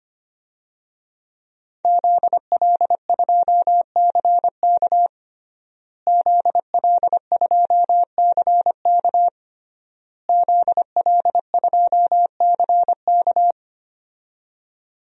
Morse code, heard as a keyed tone, about 25 wpm, 700 Hz. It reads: ZL2CK ZL2CK ZL2CK
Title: DXCC-100 Call Signs - 5 Characters - Repeated 3x - 25wpm